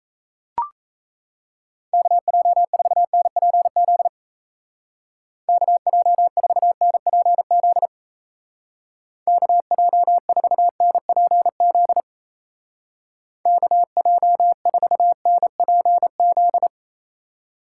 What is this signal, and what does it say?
Morse code, heard as a keyed tone, about 34 wpm, 700 Hz. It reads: KJ4NPZ KJ4NPZ KJ4NPZ KJ4NPZ
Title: Speed Racing - USA 2x3 Call Signs 28wpm